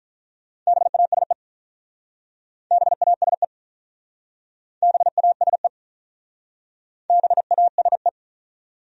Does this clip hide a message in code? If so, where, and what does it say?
Yes: Morse code from 0.7 to 8.1 s, heard as a keyed tone, about 46 wpm, 700 Hz. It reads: BASE BASE BASE BASE